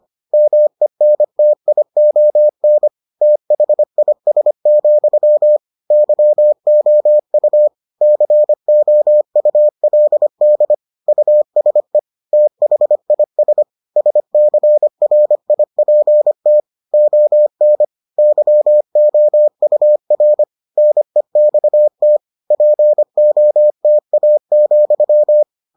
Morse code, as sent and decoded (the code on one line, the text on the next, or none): -- . -. - .. --- -. / - .... .. ... --..-- / -.-- --- ..- / -.-. --- ..- .-.. -.. / ..- ... / - .... .. ... / ... -.-. .-. .. .--. - / --- -. / -.-- --- ..- .-. / -. -..- - / .--. --- - .- --..--
MENTION THIS, YOU COULD USE THIS SCRIPT ON YOUR NEXT POTA,